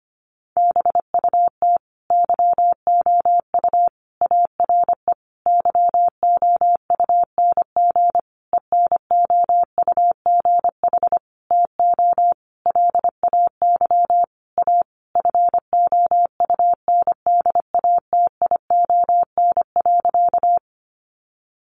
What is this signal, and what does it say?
Morse code, heard as a keyed tone, about 25 wpm, 700 Hz. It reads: BUT YOU ARE YOUNG ENOUGH TO LAY A FOUNDATION.
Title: Sentences from Top 1200 Words 25wpm